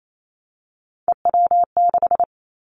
Morse code, heard as a keyed tone, about 28 wpm, 700 Hz. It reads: EW6
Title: Callsign Prefixes for Top 100 DXCC - 28wpm